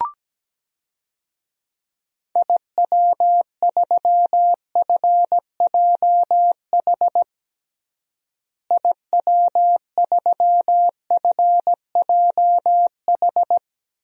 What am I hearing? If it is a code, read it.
Morse code, heard as a keyed tone, about 17 wpm, 700 Hz. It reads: IW3FJH IW3FJH
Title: DXCC-100 Call Signs - 6 Characters - Repeated 3x - 17wpm